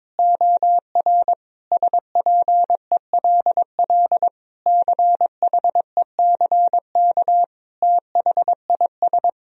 Morse code, heard as a keyed tone, about 22 wpm, 700 Hz. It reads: OR SPELL CHECK THIS
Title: Sentences from Top 1000 Words - Review Part 3 - 22wpm